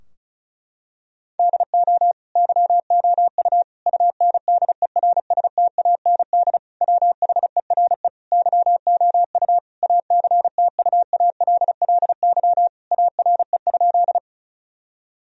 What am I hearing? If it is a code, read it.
Morse code, heard as a keyed tone, about 36 wpm, 700 Hz. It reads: DO YOU UNDERSTAND WHERE YOU ACTUALLY ARE?
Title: Sentences from Top 1000 Words - Review Part 4 - 35wpm